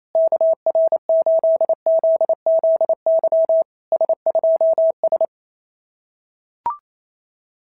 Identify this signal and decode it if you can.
Morse code, heard as a keyed tone, about 28 wpm, 650 Hz. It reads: KR8ZZY S2S
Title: SOTA QSO Practice 28wpm